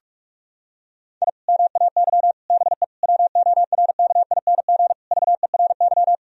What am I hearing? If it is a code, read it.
Morse code, heard as a keyed tone, about 43 wpm, 700 Hz. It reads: I MAY BE WORKING VERY
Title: Sentences from Top 1000 Words - Review Part 4 - 45wpm